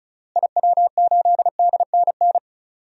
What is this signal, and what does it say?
Morse code, heard as a keyed tone, about 36 wpm, 700 Hz. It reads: IW8DNN
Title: DXCC-100 Call Signs - 6 Characters - Repeated 3x - 35wpm